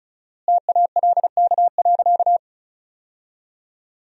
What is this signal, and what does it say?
Morse code, heard as a keyed tone, about 34 wpm, 700 Hz. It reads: TALK.
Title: 4x Word-Spacing - Sentences from Top 400 Words - 35wpm